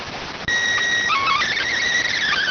SSTV (slow-scan television), PD 290.